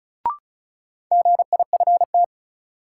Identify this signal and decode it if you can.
Morse code, heard as a keyed tone, about 34 wpm, 700 Hz. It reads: GIFT
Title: Sentences from Top 900 Words - New Word Drill - 35wpm